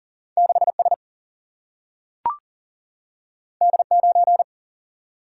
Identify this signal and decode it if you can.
Morse code, heard as a keyed tone, about 38 wpm, 700 Hz. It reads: 6 S D 9
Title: Single-Letter Single-Number 40wpm